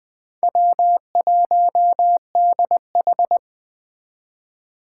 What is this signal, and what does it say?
Morse code, heard as a keyed tone, about 20 wpm, 700 Hz. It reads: W1DH